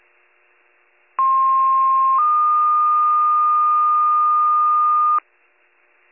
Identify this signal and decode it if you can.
Two-tone page, heard as a one-second tone, then a three-second tone, about 1060 then 1220 Hz. It is heard from 1.2 to 5.2 s.